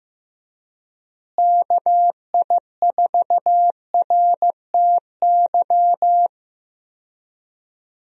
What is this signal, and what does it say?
Morse code, heard as a keyed tone, about 15 wpm, 700 Hz. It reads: KI4RTY
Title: USA Call Signs - 6 Characters - Repeated 2x - 15wpm